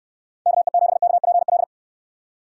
Morse code, heard as a keyed tone, about 69 wpm, 700 Hz. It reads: KE7KPL